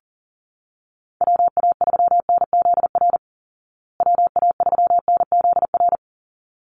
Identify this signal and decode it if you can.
Morse code, heard as a keyed tone, about 40 wpm, 700 Hz. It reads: WA3NZR WA3NZR
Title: Call Signs Repeated 3x - 40wpm